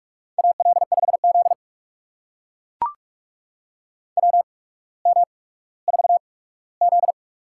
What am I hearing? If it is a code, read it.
Morse code, heard as a keyed tone, about 44 wpm, spaced as for 19 wpm, 690 Hz. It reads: AL5Z WM4Z
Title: USA Call Signs - 4 Characters - 8x Character Spacing - 45wpm